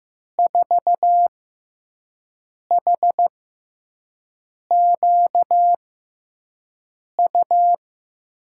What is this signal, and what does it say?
Morse code, heard as a keyed tone, about 15 wpm, spaced as for 5 wpm, 700 Hz. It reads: 4HQU
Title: USA Call Signs - 5 Characters - 6x Character Spacing - 15wpm